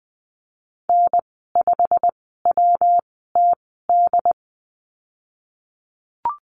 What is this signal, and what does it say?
Morse code, heard as a keyed tone, about 20 wpm, spaced as for 14 wpm, 700 Hz. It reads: N5WTD
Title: 154 Lesson 25g - Letter p - Call Signs 20wpm